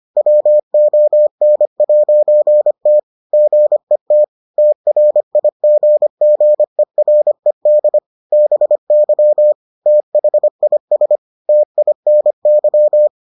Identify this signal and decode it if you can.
Morse code, heard as a keyed tone, about 25 wpm, 600 Hz. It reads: WON'T GET TRIGGERED BY THIS TINY